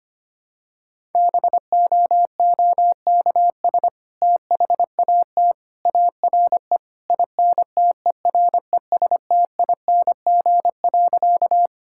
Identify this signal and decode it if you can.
Morse code, heard as a keyed tone, about 25 wpm, 700 Hz. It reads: BOOKS THAT ARE INTERESTING.
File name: practice / 25wpm / Sentences with 2-Word Phrases - 25wpm